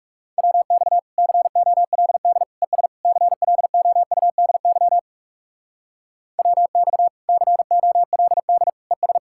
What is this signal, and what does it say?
Morse code, heard as a keyed tone, about 37 wpm, 700 Hz. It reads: WX COLD ES CLOUDY WX COLD ES
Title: Speed-Racing Common QSO Phrases Encore - 30wpm